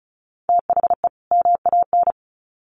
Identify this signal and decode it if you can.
Morse code, heard as a keyed tone, about 34 wpm, 700 Hz. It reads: THE MAN
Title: Sets of 2 Words 35wpm